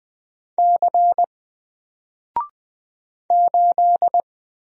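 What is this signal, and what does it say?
Morse code, heard as a keyed tone, about 20 wpm, 700 Hz. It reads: C 8
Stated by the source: Single letter and number Morse drill at 20 wpm, from N4LCD.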